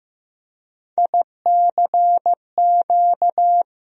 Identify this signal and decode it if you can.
Morse code, heard as a keyed tone, about 15 wpm, 700 Hz. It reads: ICQ
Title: USA Call Sign Postfixes - 3 Letters 15wpm